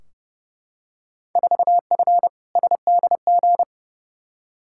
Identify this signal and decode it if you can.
Morse code, heard as a keyed tone, about 30 wpm, 700 Hz. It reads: 4F SDG
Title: Field Day Exchanges - 30wpm